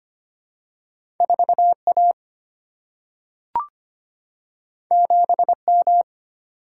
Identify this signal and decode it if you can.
Morse code, heard as a keyed tone, about 25 wpm, 690 Hz. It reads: 4A 7M
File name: Two Letter-Number Combinations - 25wpm